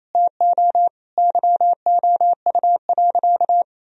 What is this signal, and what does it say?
Morse code, heard as a keyed tone, about 28 wpm, 700 Hz. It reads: TO YOU.
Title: Sentences from Top 1000 Words - Review Part 2 - 28wpm